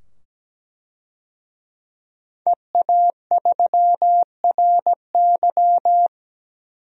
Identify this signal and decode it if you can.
Morse code, heard as a keyed tone, about 17 wpm, 700 Hz. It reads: EA3RY
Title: DXCC-100 Call Signs - 5 Characters - 17wpm